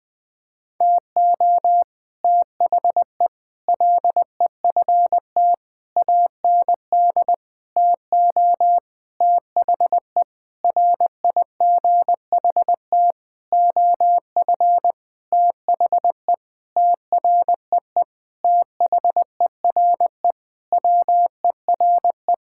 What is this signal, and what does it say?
Morse code, heard as a keyed tone, about 21 wpm, 700 Hz. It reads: TO THE LEFT AND TO THE RIGHT OF THE TREE THERE WERE
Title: Sentences with 9-Word Phrases - 20wpm